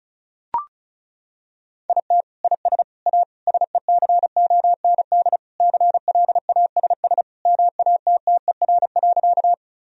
Morse code, heard as a keyed tone, about 34 wpm, 700 Hz. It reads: IT IS A SECOND CLASS MATTER.
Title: Sentences from Top 1000 Words - Review Part 2 - 35wpm